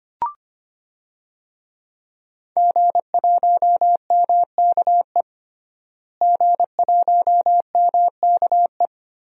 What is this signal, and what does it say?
Morse code, heard as a keyed tone, about 25 wpm, 700 Hz. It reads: G1MKE G1MKE
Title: DXCC-100 Call Signs - 5 Characters - Repeated 3x - 25wpm